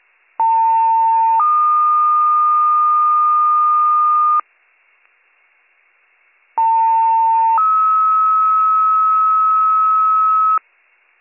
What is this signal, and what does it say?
Two-tone page, heard as a one-second tone, then a three-second tone, about 900 then 1220 Hz, about 900 then 1290 Hz. It is heard from 0.4 to 10.6 s.